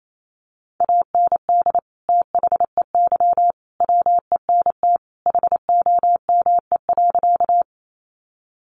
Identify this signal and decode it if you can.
Morse code, heard as a keyed tone, about 28 wpm, 700 Hz. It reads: AND THEY WENT HOME.
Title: Sentences from Top 200 Words 28wpm